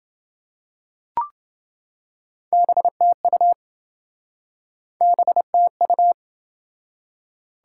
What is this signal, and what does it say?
Morse code, heard as a keyed tone, about 28 wpm, 700 Hz. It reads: BTU BTU